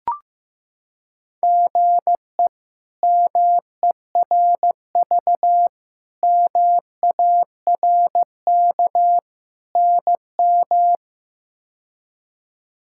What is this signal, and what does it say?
Morse code, heard as a keyed tone, about 15 wpm, 700 Hz. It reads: GE MERV MARK NM
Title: K1USN SST QSO Practice - 15wpm